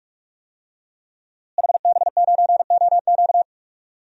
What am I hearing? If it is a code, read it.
Morse code, heard as a keyed tone, about 46 wpm, 700 Hz. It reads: HB9OQ
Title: DXCC-100 Call Signs - 5 Characters - Repeated 3x - 45wpm